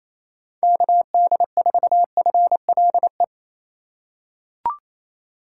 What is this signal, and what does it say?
Morse code, heard as a keyed tone, about 27 wpm, 700 Hz. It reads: KD4FLE
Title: USA Call Signs - 6 Characters - 4x Character Spacing - 28wpm